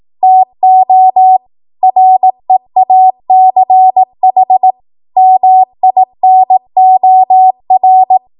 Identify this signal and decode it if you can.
Morse code, heard as a keyed tone, about 18 wpm, 750 Hz. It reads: TO REACH MINOR
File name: Morse / 18WPM